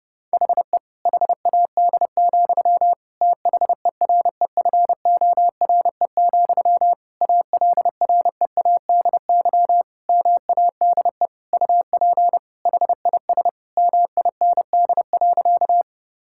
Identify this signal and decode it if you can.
Morse code, heard as a keyed tone, about 32 wpm, 700 Hz. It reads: HE HAD, THEREFORE, ALREADY MADE UP HIS MIND.
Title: Sentences from Top 1000 Words - Review Part 5 - 30wpm